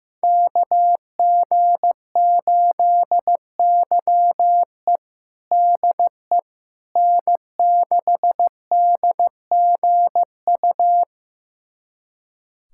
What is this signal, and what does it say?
Morse code, heard as a keyed tone, about 15 wpm, 700 Hz. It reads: KG8YE DE N6DGU
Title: USA Callsign DE Callsign - 5 Characters - 15wpm